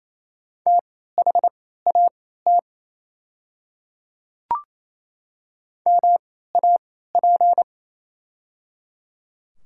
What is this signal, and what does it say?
Morse code, heard as a keyed tone, about 28 wpm, spaced as for 16 wpm, 690 Hz. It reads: THAT MAP
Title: 3x Character Spacing - Top 500 Words - 28wpm